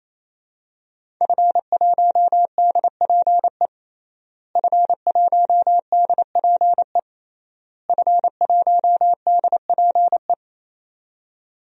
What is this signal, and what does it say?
Morse code, heard as a keyed tone, about 27 wpm, 700 Hz. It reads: F1DPE F1DPE F1DPE
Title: DXCC-100 Call Signs - 5 Characters - Repeated 3x - 28wpm